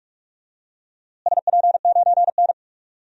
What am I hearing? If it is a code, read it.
Morse code, heard as a keyed tone, about 46 wpm, 700 Hz. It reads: SP9N
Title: DXCC-100 Call Signs - 4 Characters - Repeated 2x - 45wpm